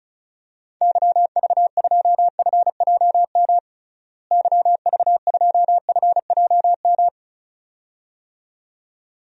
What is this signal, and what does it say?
Morse code, heard as a keyed tone, about 34 wpm, 700 Hz. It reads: YV2FJM YV2FJM
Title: DXCC-100 Call Signs - 6 Characters - Repeated 2x - 35wpm